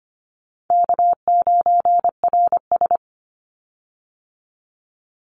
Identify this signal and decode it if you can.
Morse code, heard as a keyed tone, about 25 wpm, 700 Hz. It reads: K9RS
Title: Call Signs Repeated 2x - 25wpm